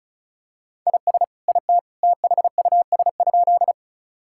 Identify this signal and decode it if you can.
Morse code, heard as a keyed tone, about 34 wpm, 700 Hz. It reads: IS IT THUS?
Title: Sentences from Top 1000 Words - Review Part 5 - 35wpm